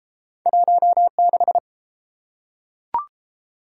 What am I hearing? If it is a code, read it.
Morse code, heard as a keyed tone, about 29 wpm, 700 Hz. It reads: 16